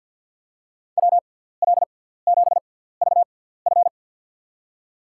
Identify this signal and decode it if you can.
Morse code, heard as a keyed tone, about 48 wpm, spaced as for 17 wpm, 700 Hz. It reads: WL7VF
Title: USA Call Signs - 5 Characters - 6x Character Spacing - 50wpm